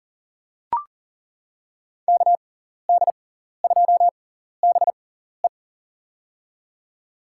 Morse code, heard as a keyed tone, about 40 wpm, spaced as for 14 wpm, 700 Hz. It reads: KD2BE
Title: USA Call Signs - 6x Character Spacing - 40wpm